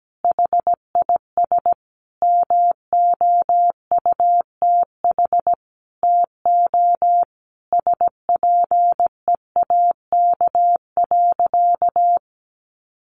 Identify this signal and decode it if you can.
Morse code, heard as a keyed tone, about 17 wpm, 700 Hz. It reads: HIS MOUTH TO SPEAK.
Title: Sentences from Top 1500 Words 17wpm